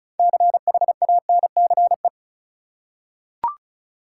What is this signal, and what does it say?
Morse code, heard as a keyed tone, about 34 wpm, 700 Hz. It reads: CHANCE
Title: Six Letter Words - 35wpm